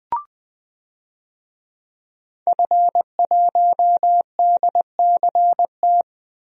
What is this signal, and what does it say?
Morse code, heard as a keyed tone, about 20 wpm, 700 Hz. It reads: F1DCT
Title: DXCC-100 Call Signs - 5 Characters - 20wpm